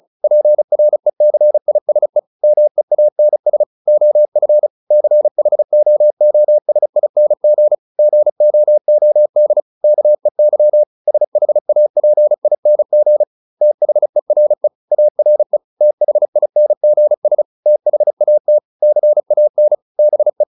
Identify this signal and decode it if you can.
Morse code, heard as a keyed tone, about 34 wpm, 600 Hz. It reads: PRECISE MEANS OF CHOOSING GOOD KEY SHAPING THERE ARE THINGS THAT CAN BE